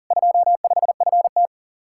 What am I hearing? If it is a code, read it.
Morse code, heard as a keyed tone, about 53 wpm, 700 Hz. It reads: 25FT